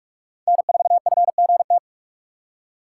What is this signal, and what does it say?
Morse code, heard as a keyed tone, about 44 wpm, 700 Hz. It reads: N4FGT